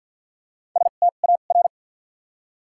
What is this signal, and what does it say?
Morse code, heard as a keyed tone, about 46 wpm, spaced as for 35 wpm, 700 Hz. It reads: STAR
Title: 2x Character Spacing - Top 500 Words - 50wpm